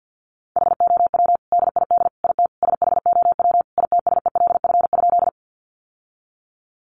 Morse code, heard as a keyed tone, about 50 wpm, 700 Hz. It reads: HOW DID IT SHOW ITSELF?
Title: Sentences from Top 800 Words 50wpm